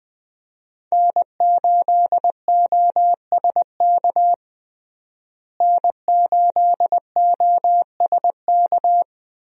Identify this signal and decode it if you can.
Morse code, heard as a keyed tone, about 20 wpm, 700 Hz. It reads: N8OSK N8OSK
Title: USA Call Signs - 5 Characters - Repeated 2x - 20wpm